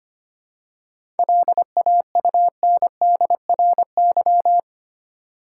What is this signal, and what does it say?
Morse code, heard as a keyed tone, about 25 wpm, 700 Hz. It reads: LAUNDRY
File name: practice / 25wpm / Easy Words with Hints - 25wpm